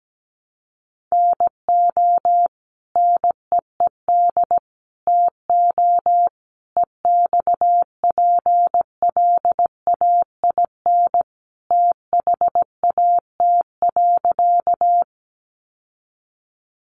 Morse code, heard as a keyed tone, about 17 wpm, 700 Hz. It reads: NO NEED TO EXPLAIN THAT.
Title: Sentences from Top 1100 Words 17wpm